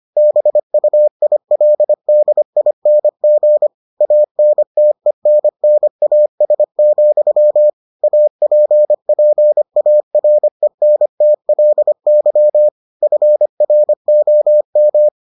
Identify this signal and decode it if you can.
Morse code, heard as a keyed tone, about 25 wpm, 600 Hz. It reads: BUILDING ANTENNAS, APPARENTLY FROM